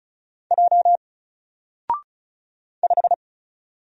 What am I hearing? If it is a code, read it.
Morse code, heard as a keyed tone, about 35 wpm, 700 Hz. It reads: J 5
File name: Single-Letter-Number-35wpm